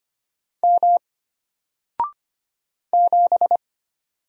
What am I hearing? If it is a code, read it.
Morse code, heard as a keyed tone, about 25 wpm, 700 Hz. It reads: M 7
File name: Single-Letter-Number-25wpm